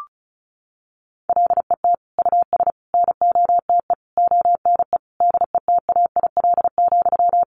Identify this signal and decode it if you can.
Morse code, heard as a keyed tone, about 34 wpm, 700 Hz. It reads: LET US NOTE ONE DETAIL,
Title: Sentences from Top 1500 Words 35wpm